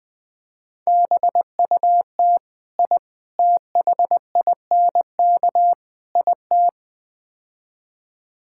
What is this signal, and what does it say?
Morse code, heard as a keyed tone, about 20 wpm, 700 Hz. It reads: BUT I THINK IT
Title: Build A Phrase - 4-Word Phrases - 20wpm